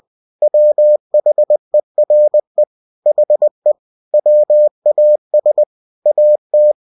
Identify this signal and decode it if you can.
Morse code, heard as a keyed tone, about 20 wpm, 600 Hz. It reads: WHERE HE WAS AT
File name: cw-books / winnie-the-pooh / winnie-20wpm-page-0027